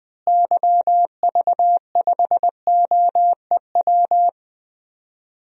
Morse code, heard as a keyed tone, about 21 wpm, 700 Hz. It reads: YV5OEW
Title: DXCC-100 Call Signs - 6 Characters - Repeated 2x - 20wpm